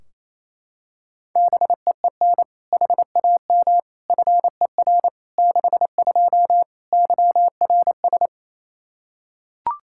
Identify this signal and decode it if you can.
Morse code, heard as a keyed tone, about 28 wpm, 700 Hz. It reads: BEEN HAM FER 62 YRS
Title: K1USN SST QSO Practice 28wpm